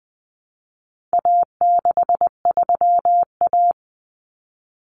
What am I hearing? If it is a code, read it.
Morse code, heard as a keyed tone, about 20 wpm, 700 Hz. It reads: A63A
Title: Common Cut Numbers - 4-Digit 20wpm